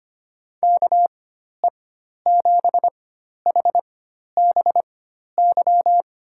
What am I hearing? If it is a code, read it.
Morse code, heard as a keyed tone, about 25 wpm, spaced as for 12 wpm, 700 Hz. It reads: KE7HBY
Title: USA Call Signs - 6 Characters - 4x Character Spacing - 25wpm